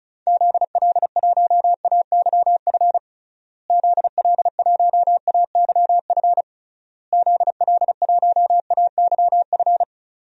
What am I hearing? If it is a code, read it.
Morse code, heard as a keyed tone, about 36 wpm, 700 Hz. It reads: ZL1AYF ZL1AYF ZL1AYF
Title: DXCC-100 Call Signs - 6 Characters - Repeated 3x - 35wpm